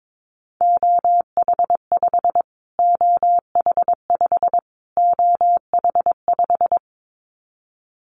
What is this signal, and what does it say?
Morse code, heard as a keyed tone, about 22 wpm, 700 Hz. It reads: OH5 OH5 OH5
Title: Callsign Prefixes for Top 25 DXCC - Repeated 3x - 22wpm